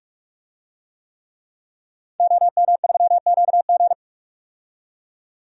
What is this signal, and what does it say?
Morse code, heard as a keyed tone, about 46 wpm, 700 Hz. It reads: OM3QG
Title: DXCC-100 Call Signs - 5 Characters - 45wpm